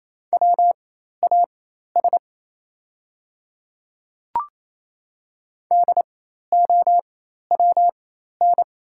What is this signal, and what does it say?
Morse code, heard as a keyed tone, about 27 wpm, spaced as for 13 wpm, 690 Hz. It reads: WAS DOWN
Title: 4x Character Spacing - Top 100 Words - 28wpm